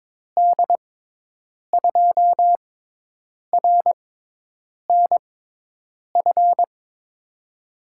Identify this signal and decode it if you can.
Morse code, heard as a keyed tone, about 22 wpm, spaced as for 8 wpm, 700 Hz. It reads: D2RNF
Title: USA Call Signs - 6 Characters - 6x Character Spacing - 22wpm